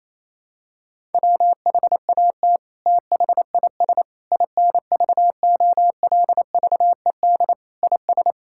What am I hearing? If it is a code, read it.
Morse code, heard as a keyed tone, about 28 wpm, 700 Hz. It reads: WHAT THIS INVOLVED IS
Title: Sentences from Top 1000 Words - Review Part 5 - 28wpm